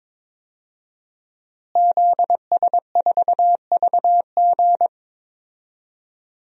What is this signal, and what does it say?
Morse code, heard as a keyed tone, about 22 wpm, 700 Hz. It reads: ZS4VG
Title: DXCC-100 Call Signs - 5 Characters - Repeated 3x - 22wpm